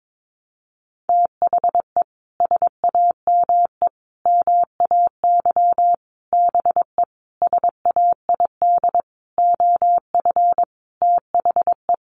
Morse code, heard as a keyed tone, about 22 wpm, 700 Hz. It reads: THE SAME MAY BE SAID OF THE
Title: Sentences from Top 1100 Words 22wpm